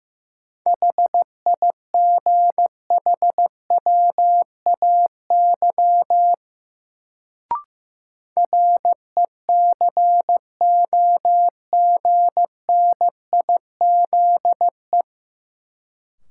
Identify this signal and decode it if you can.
Morse code, heard as a keyed tone, about 15 wpm, 700 Hz. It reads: HIGHWAY RECOGNIZE
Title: Sentences from Top 1400 Words - New Word Drill Alt - 15wpm